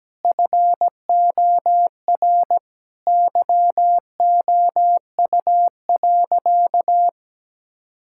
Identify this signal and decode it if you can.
Morse code, heard as a keyed tone, about 17 wpm, 700 Hz. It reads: FOR YOU.
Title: Sentences from Top 1600 Words 17wpm